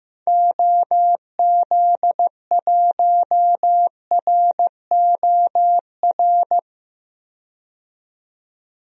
Morse code, heard as a keyed tone, about 15 wpm, 700 Hz. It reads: OZ1ROR
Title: DXCC-100 Call Signs - 6 Characters - 15wpm